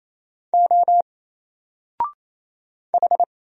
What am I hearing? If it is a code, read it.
Morse code, heard as a keyed tone, about 27 wpm, 700 Hz. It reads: O H
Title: Single Letter-Number Alt 28wpm